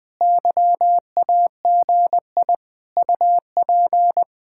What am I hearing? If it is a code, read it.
Morse code, heard as a keyed tone, about 20 wpm, 700 Hz. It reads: YAGI UP